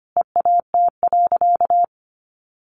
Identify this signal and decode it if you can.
Morse code, heard as a keyed tone, about 25 wpm, 700 Hz. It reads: EAT.
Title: Sentences from Top 700 Words 25wpm